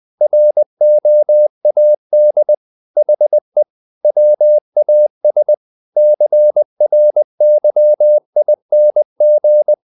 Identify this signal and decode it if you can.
Morse code, heard as a keyed tone, about 20 wpm, 600 Hz. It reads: ROAD HE WAS CRYING
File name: the old man and the sea part 2 20wpm